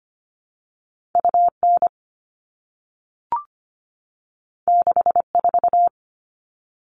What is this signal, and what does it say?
Morse code, heard as a keyed tone, about 25 wpm, 700 Hz. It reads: UN 64
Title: All Cut Numbers - 2-Digit 25wpm